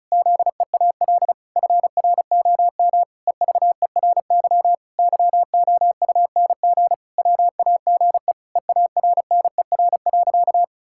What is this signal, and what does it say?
Morse code, heard as a keyed tone, about 34 wpm, 700 Hz. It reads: ZEAL FROM EVERY YOUNG WAGE EARNER.
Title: Pangram Challenge v2 - 35wpm